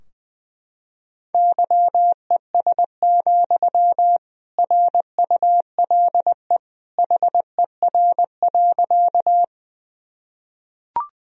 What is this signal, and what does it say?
Morse code, heard as a keyed tone, about 20 wpm, 700 Hz. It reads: YES, RULE HER.
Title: Sentences from Top 1000 Words - Review Part 2 - 20wpm